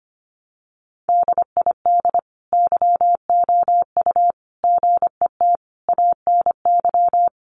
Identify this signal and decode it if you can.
Morse code, heard as a keyed tone, about 25 wpm, 700 Hz. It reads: DID YOU GET ANY
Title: Sentences from Top 400 Words 25wpm